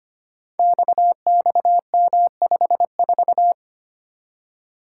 Morse code, heard as a keyed tone, about 25 wpm, 700 Hz. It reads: XXM54
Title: Code Groups - 5 Letter:Numbers - 25wpm